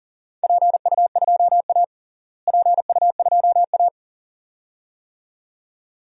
Morse code, heard as a keyed tone, about 40 wpm, 700 Hz. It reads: PU2A PU2A
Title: DXCC-100 Call Signs - 4 Characters - Repeated 2x - 40wpm